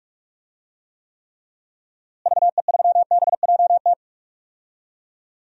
Morse code, heard as a keyed tone, about 46 wpm, 700 Hz. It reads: VE3BJT
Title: DXCC-100 Call Signs - 6 Characters - Repeated 3x - 45wpm